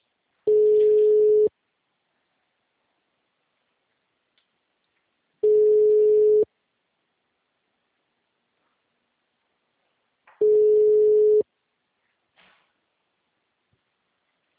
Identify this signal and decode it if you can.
Ringback tone (Europe, 425 Hz).